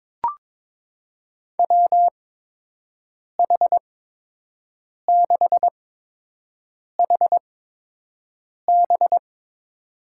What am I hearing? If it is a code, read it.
Morse code, heard as a keyed tone, about 22 wpm, spaced as for 6 wpm, 700 Hz. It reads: WH6HB
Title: USA Call Signs - 5 Characters - 8x Character Spacing - 22wpm